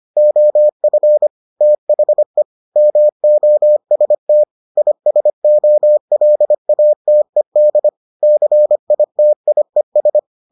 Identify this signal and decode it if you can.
Morse code, heard as a keyed tone, about 25 wpm, 600 Hz. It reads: OF THE MOST ISOLATED CITIES